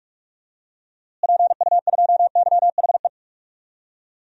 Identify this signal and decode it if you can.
Morse code, heard as a keyed tone, about 46 wpm, 700 Hz. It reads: PU2YHE
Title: DXCC-100 Call Signs - 6 Characters - Repeated 2x - 45wpm